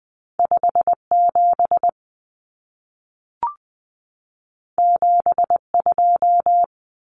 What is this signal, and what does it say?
Morse code, heard as a keyed tone, about 20 wpm, 700 Hz. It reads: 57 72
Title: Common Cut Numbers - 2-Digit 20wpm